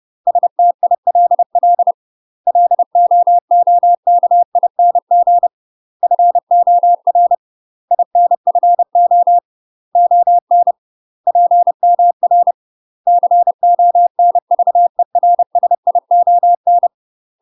Morse code, heard as a keyed tone, about 30 wpm, 700 Hz. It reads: STILL LOOKING FOR INFO ON PMR CONVERSION
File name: UK MASTER QSOs TRAINING / 30WPM / UK MASTER QSOs 0048